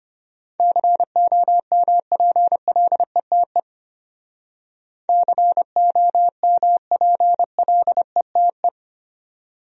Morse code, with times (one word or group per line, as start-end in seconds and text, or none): 0.6-3.6 COMPLETE
5.1-8.7 COMPLETE